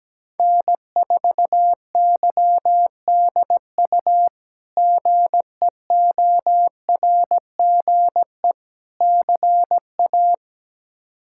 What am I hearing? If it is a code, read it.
Morse code, heard as a keyed tone, about 17 wpm, 700 Hz. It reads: N4YDU GEORGE CA
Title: K1USN SST QSO Practice - 17wpm